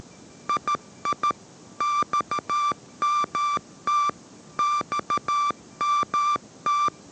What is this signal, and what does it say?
Morse code, heard as a keyed tone, about 13 wpm, 1180 Hz. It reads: II XMT XMT